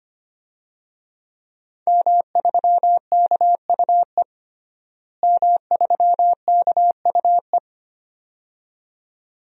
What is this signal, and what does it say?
Morse code, heard as a keyed tone, about 25 wpm, 700 Hz. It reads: M3KUE M3KUE
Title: DXCC-100 Call Signs - 5 Characters - Repeated 2x - 25wpm